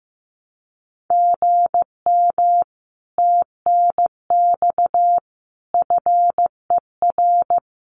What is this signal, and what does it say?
Morse code, heard as a keyed tone, about 15 wpm, 700 Hz. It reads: GM TNX FER
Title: Common QSO Phrases Encore 15wpm